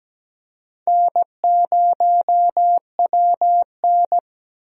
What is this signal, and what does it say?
Morse code, heard as a keyed tone, about 17 wpm, 700 Hz. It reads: N0WN
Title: USA Call Signs - 4 Characters - 17wpm